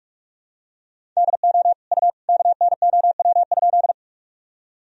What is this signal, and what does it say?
Morse code, heard as a keyed tone, about 46 wpm, 700 Hz. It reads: DO U KNOW?
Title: Sentences from Top 100 Words and Abbreviations 45wpm